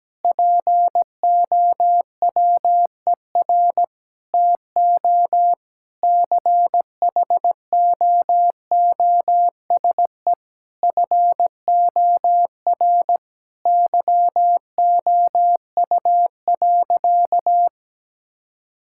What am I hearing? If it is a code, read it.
Morse code, heard as a keyed tone, about 17 wpm, 700 Hz. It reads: POWER TO CHOOSE FOR YOU.
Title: Sentences from Top 1000 Words - Review Part 6 - 17wpm